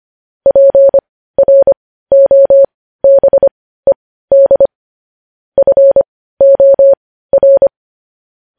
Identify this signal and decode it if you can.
Morse code, heard as a keyed tone, about 25 wpm, spaced as for 15 wpm, 550 Hz. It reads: PROBED FOR
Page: News Headlines in Morse Code at 15 WPM